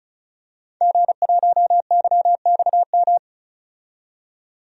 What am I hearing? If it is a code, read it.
Morse code, heard as a keyed tone, about 36 wpm, 700 Hz. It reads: G1YXM
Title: DXCC-100 Call Signs - 5 Characters - 35wpm